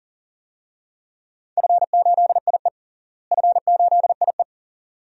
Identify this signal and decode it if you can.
Morse code, heard as a keyed tone, about 37 wpm, 700 Hz. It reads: F8IE F8IE
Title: DXCC-100 Call Signs - 4 Characters - Repeated 2x - 40wpm